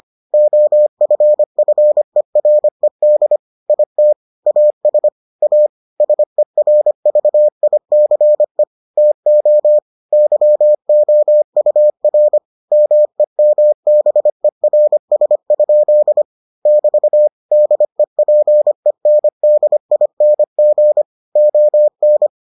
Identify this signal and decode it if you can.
Morse code, heard as a keyed tone, about 25 wpm, 600 Hz. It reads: OFFERED IT AS A SERVICE TO YOUR MEMBERS? <BT> DEPENDING ON